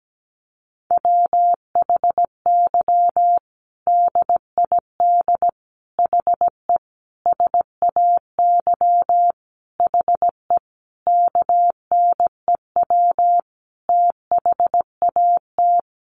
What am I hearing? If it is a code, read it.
Morse code, heard as a keyed tone, about 17 wpm, 700 Hz. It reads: WHY DID HE SAY HE KNEW THAT
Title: Sentences from Top 1500 Words 17wpm